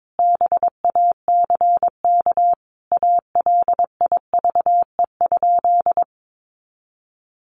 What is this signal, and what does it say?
Morse code, heard as a keyed tone, about 22 wpm, 700 Hz. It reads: BACK ALIVE?
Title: A Sound Of Thunder 22wpm